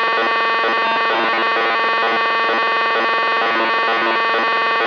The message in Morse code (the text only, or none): none